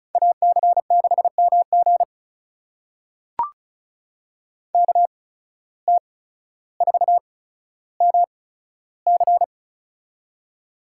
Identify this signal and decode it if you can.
Morse code, heard as a keyed tone, about 36 wpm, spaced as for 15 wpm, 700 Hz. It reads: AC6MG KT4MC
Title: USA Call Signs - 5 Characters - 8x Character Spacing - 35wpm